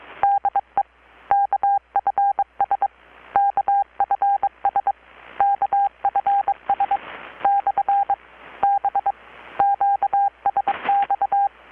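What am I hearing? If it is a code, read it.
Morse code, heard as a keyed tone, about 22 wpm, 800 Hz. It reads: DE KFS KFS KFS / B QSX
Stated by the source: Morse from KPH's Night of Nights XX maritime radiotelegraphy event, received in Chiapas, Mexico about 3,800 km away on a Kenwood R-600 with a grounded 10 m long-wire antenna.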